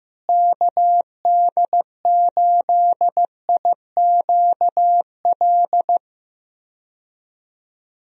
Morse code, heard as a keyed tone, about 16 wpm, 700 Hz. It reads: KD8IQL
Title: USA Call Signs - 6 Characters - Repeated 3x - 15wpm